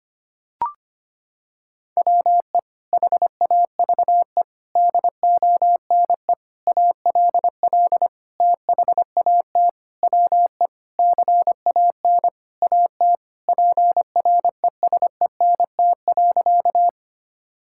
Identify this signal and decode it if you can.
Morse code, heard as a keyed tone, about 25 wpm, 700 Hz. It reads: WE HAVE DONE ALL THAT WE CAN AT PRESENT.
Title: Sentences from Top 1000 Words - Review Part 5 - 25wpm